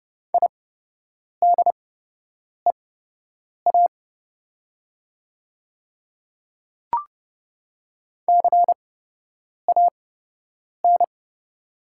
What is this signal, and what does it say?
Morse code, heard as a keyed tone, about 30 wpm, spaced as for 8 wpm, 700 Hz. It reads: IDEA CAN
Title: 8x Character Spacing - Top 300 Words - 30wpm